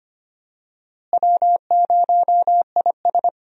Morse code, heard as a keyed tone, about 25 wpm, 700 Hz. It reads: W0IS
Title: USA Call Signs - 4 Characters - Repeated 3x - 25wpm